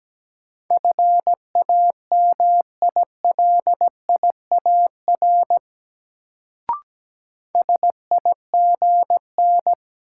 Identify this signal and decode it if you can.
Morse code, heard as a keyed tone, about 17 wpm, 700 Hz. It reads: FAMILIAR SIGN
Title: Sentences from Top 1000 Words - New Word Drill - 17wpm